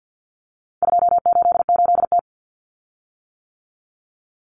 Morse code, heard as a keyed tone, about 50 wpm, 700 Hz. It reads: 288T
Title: Common Cut Numbers - 4-Digit 50wpm